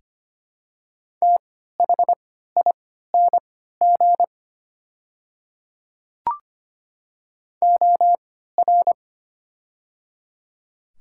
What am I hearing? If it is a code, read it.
Morse code, heard as a keyed tone, about 25 wpm, spaced as for 14 wpm, 700 Hz. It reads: THING OR
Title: 3x Character Spacing - Top 100 Words - 25wpm